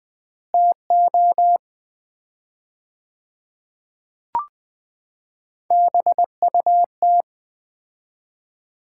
Morse code, heard as a keyed tone, about 20 wpm, 700 Hz. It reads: TO BUT
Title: 4x Word-Spacing - Sets of 4 Words - 20wpm